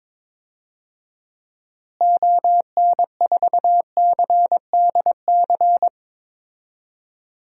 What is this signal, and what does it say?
Morse code, heard as a keyed tone, about 21 wpm, 700 Hz. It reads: ON4CDC
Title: DXCC-100 Call Signs - 6 Characters - Repeated 3x - 22wpm